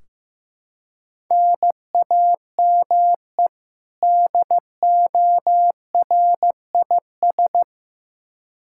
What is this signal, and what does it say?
Morse code, heard as a keyed tone, about 15 wpm, 700 Hz. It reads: NAME DORIS
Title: Practice QSO - Exchange 1 - Beginning - 15wpm